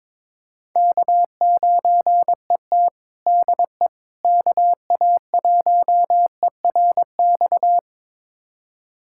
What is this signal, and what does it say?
Morse code, heard as a keyed tone, about 22 wpm, 700 Hz. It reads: K9ET DE KA1ERX
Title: USA Callsign DE Callsign 22wpm